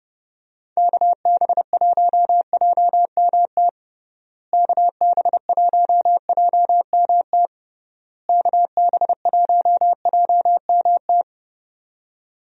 Morse code, heard as a keyed tone, about 30 wpm, 700 Hz. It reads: KB1JMT KB1JMT KB1JMT
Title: USA Call Signs - 6 Characters - Repeated 3x - 30wpm